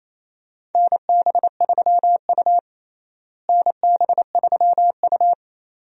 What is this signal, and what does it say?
Morse code, heard as a keyed tone, about 28 wpm, 700 Hz. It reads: NB3U NB3U
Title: USA Call Signs - 4 Characters - Repeated 2x - 28wpm